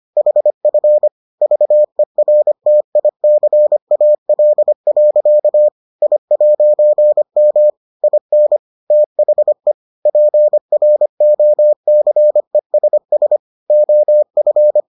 Morse code, heard as a keyed tone, about 25 wpm, 600 Hz. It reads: HF VERTICAL. I'M IN THE PROCESS OF